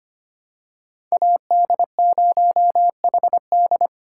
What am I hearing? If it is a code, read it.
Morse code, heard as a keyed tone, about 25 wpm, 700 Hz. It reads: AD0HD
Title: USA Call Signs - 5 Characters - 4x Character Spacing - 25wpm